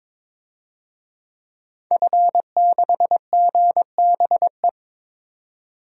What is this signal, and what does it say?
Morse code, heard as a keyed tone, about 22 wpm, 700 Hz. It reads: F6GBE
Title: DXCC-100 Call Signs - 5 Characters - 22wpm